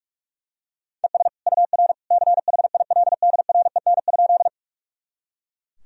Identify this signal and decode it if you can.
Morse code, heard as a keyed tone, about 42 wpm, 700 Hz. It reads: ES UR CHILDREN?
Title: Sentences from Top 300 Words and Abbreviations 45wpm